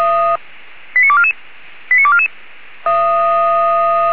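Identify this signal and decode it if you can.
Five-tone selective call (ZVEI), heard as quick runs of five tones, tones of about 70 ms. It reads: 89244 89244
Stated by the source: Five-tone ZVEI selective call received by SDR.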